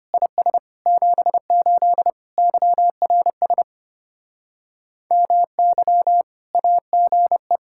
Morse code, heard as a keyed tone, about 32 wpm, 700 Hz. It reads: IS 78 YRS MY AGE